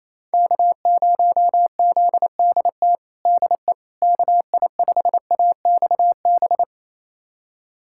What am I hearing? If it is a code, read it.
Morse code, heard as a keyed tone, about 28 wpm, 700 Hz. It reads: K0ZDT DE KI5AXB
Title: USA Callsign DE Callsign 28wpm